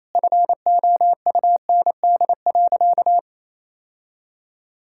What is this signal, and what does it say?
Morse code, heard as a keyed tone, about 28 wpm, 700 Hz. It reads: FOUND.
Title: Sentences from Top 1600 Words 28wpm